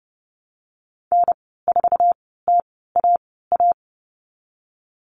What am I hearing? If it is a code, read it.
Morse code, heard as a keyed tone, about 30 wpm, spaced as for 17 wpm, 700 Hz. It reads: N4TAA
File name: Lesson 14g - Number 2 - Call Signs 30wpm